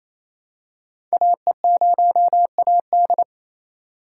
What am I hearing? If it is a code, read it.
Morse code, heard as a keyed tone, about 27 wpm, 700 Hz. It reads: AE0AD
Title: Code Groups - 5 Letter:Numbers - 28wpm